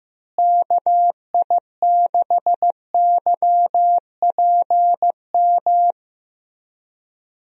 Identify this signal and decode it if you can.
Morse code, heard as a keyed tone, about 15 wpm, 700 Hz. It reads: KI6YPM
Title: USA Call Signs - 6 Characters - Repeated 3x - 15wpm